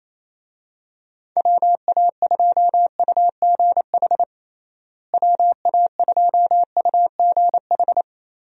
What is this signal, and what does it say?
Morse code, heard as a keyed tone, about 28 wpm, 700 Hz. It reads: WA2UGH WA2UGH
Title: USA Call Signs - 6 Characters - Repeated 2x - 28wpm